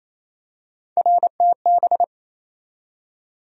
Morse code, heard as a keyed tone, about 28 wpm, 700 Hz. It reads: RTB
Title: USA Call Sign Postfixes - 3 Letters 28wpm